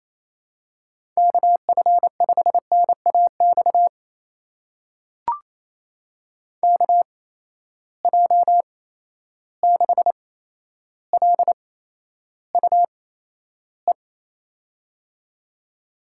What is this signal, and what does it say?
Morse code, heard as a keyed tone, about 28 wpm, spaced as for 12 wpm, 700 Hz. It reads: KF5NAX KJ6LUE
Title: USA Call Signs - 6 Characters - 8x Character Spacing - 28wpm